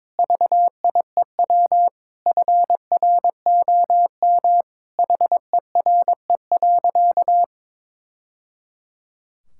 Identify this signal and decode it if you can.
Morse code, heard as a keyed tone, about 21 wpm, 700 Hz. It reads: VIEW FROM HERE.
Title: Sentences from Top 1000 Words - Review Part 6 - 22wpm